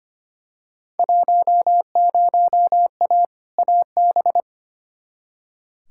Morse code, heard as a keyed tone, about 25 wpm, 700 Hz. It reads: 10A AB
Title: Field Day Exchanges Encore - 25wpm